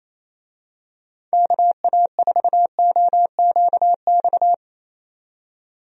Morse code, heard as a keyed tone, about 28 wpm, 700 Hz. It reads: KA4OQX
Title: USA Call Signs - 6 Characters - 28wpm